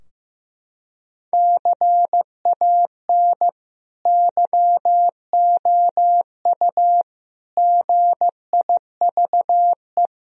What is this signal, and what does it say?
Morse code, heard as a keyed tone, about 15 wpm, 700 Hz. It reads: CAN YOU GIVE
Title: Sentences from Top 1000 Words - Review Part 4 - 15wpm